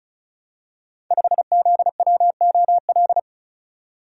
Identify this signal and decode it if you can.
Morse code, heard as a keyed tone, about 36 wpm, 700 Hz. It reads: 5ZWOL
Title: Code Groups - 5 Letter:Numbers - 35wpm